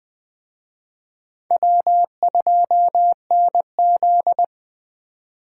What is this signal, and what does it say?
Morse code, heard as a keyed tone, about 23 wpm, 700 Hz. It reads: W2NZ
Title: Speed Racing - USA Call Signs 20wpm